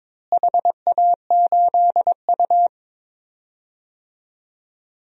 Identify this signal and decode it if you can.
Morse code, heard as a keyed tone, about 22 wpm, 700 Hz. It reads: HA8U
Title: DXCC-100 Call Signs - 4 Characters - Repeated 3x - 22wpm